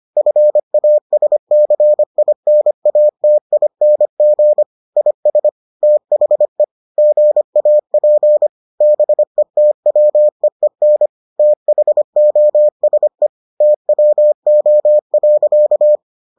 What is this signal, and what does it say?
Morse code, heard as a keyed tone, about 25 wpm, 600 Hz. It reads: FASCINATING IS THE GAP BETWEEN THOSE TWO.